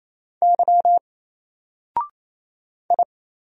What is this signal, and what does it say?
Morse code, heard as a keyed tone, about 27 wpm, 700 Hz. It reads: Y I
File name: Single-Letter-Number-28wpm